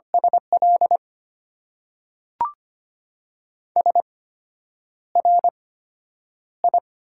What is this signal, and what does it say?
Morse code, heard as a keyed tone, about 25 wpm, spaced as for 9 wpm, 700 Hz. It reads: SL SRI
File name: Top 90 QSO Elements - 8x Character Spacing - 25wpm